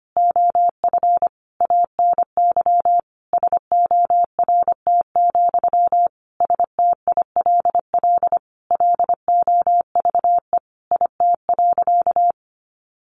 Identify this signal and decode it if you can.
Morse code, heard as a keyed tone, about 25 wpm, 700 Hz. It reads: OF ANY SORT, STILL LOVE IT.
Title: Sentences from Top 1500 Words 25wpm